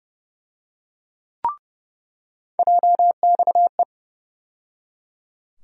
Morse code, heard as a keyed tone, about 30 wpm, 700 Hz. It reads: JXE
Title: USA Call Sign Postfixes - 3 Letters 30wpm